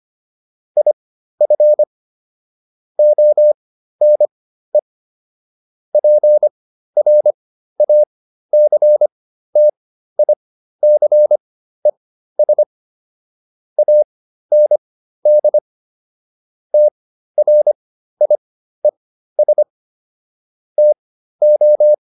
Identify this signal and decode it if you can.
Morse code, heard as a keyed tone, about 25 wpm, spaced as for 13 wpm, 600 Hz. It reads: IF ONE PRACTICES AND TRIES TO